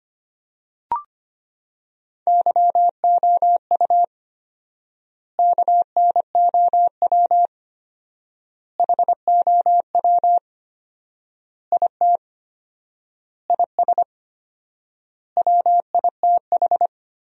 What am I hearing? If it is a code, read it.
Morse code, heard as a keyed tone, about 25 wpm, 700 Hz. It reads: YOU KNOW HOW IT IS WITH
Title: 4x Word-Spacing - Sentences from Top 100 Words - 25wpm